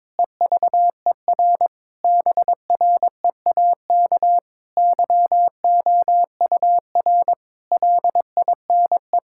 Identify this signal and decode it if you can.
Morse code, heard as a keyed tone, about 22 wpm, 700 Hz. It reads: EVER BREAK YOUR LINE
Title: Sentences from Top 1000 Words - Review Part 6 - 22wpm